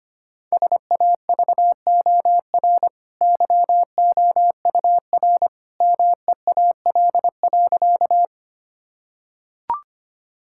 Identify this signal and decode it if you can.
Morse code, heard as a keyed tone, about 25 wpm, 700 Hz. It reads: SAVOR YOUR MEAL.
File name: Easy Sentences with Hints - 25wpm